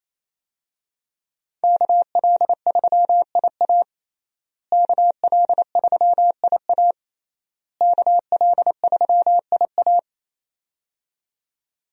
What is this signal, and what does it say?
Morse code, heard as a keyed tone, about 28 wpm, 700 Hz. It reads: KL3IA KL3IA KL3IA
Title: USA Call Signs - 5 Characters - Repeated 3x - 28wpm